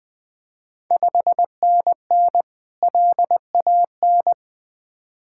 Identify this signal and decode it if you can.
Morse code, heard as a keyed tone, about 20 wpm, 700 Hz. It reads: 5NN LAN
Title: Ontario QSO Party Exchange 20wpm